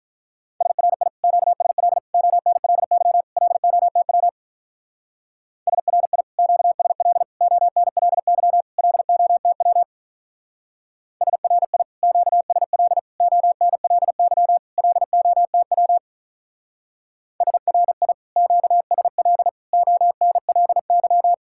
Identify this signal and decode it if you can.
Morse code, heard as a keyed tone, about 44 wpm, 700 Hz. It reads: SRI QSL ONLY LOTW SRI QSL ONLY LOTW SRI QSL ONLY LOTW SRI QSL ONLY